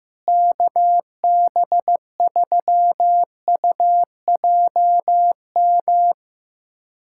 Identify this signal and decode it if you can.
Morse code, heard as a keyed tone, about 15 wpm, 700 Hz. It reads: KB3UJM
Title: USA Callsign DE Callsign 15wpm